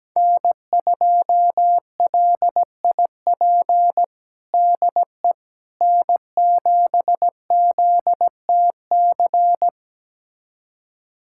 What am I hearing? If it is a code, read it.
Morse code, heard as a keyed tone, about 17 wpm, 700 Hz. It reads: N2LIP DE N7ZTC
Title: USA Callsign DE Callsign - 5 Characters - 17wpm